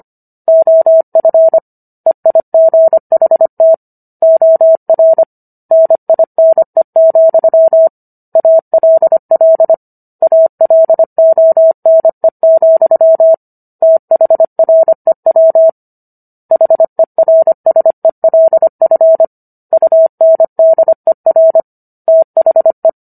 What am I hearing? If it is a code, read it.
Morse code, heard as a keyed tone, about 25 wpm, 650 Hz. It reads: OF EIGHT OR NINE, ALL ALONE, THREW HERSELF UNDER THE